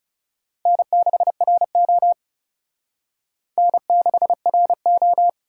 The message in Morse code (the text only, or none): N6RO N6RO